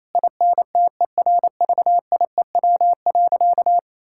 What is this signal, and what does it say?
Morse code, heard as a keyed tone, about 28 wpm, 700 Hz. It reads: INTERVIEW.
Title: Sentences from Top 1600 Words 28wpm